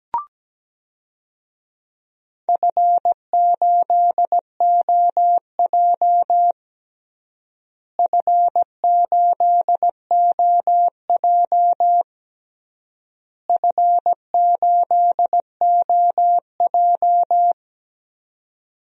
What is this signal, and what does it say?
Morse code, heard as a keyed tone, about 17 wpm, 700 Hz. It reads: F8OJ F8OJ F8OJ
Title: DXCC-100 Call Signs - 4 Characters - Repeated 3x - 17wpm